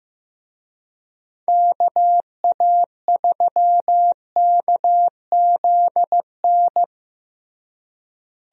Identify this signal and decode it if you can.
Morse code, heard as a keyed tone, about 15 wpm, 700 Hz. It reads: KA3KZN
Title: USA Call Signs - 6 Characters - Repeated 3x - 15wpm